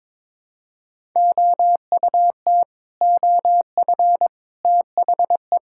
Morse code, heard as a keyed tone, about 22 wpm, 700 Hz. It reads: OUT OF THE
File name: practice / 22wpm / Build A Phrase - 5-Word Phrases - 22wpm